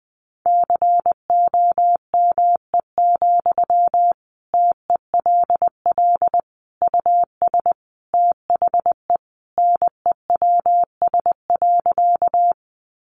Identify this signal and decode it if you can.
Morse code, heard as a keyed tone, about 20 wpm, 700 Hz. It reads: COME, TELL US THE NEWS.
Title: Sentences from Top 800 Words 20wpm